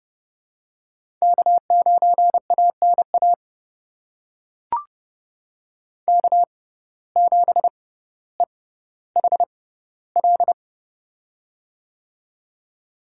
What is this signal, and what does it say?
Morse code, heard as a keyed tone, about 30 wpm, spaced as for 15 wpm, 700 Hz. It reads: K9ANA K7EHL
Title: USA Call Signs - 5 Characters - 6x Character Spacing - 30wpm